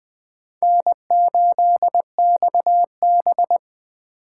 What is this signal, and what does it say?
Morse code, heard as a keyed tone, about 24 wpm, 700 Hz. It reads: N8XB
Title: Speed Racing - USA Call Signs 20wpm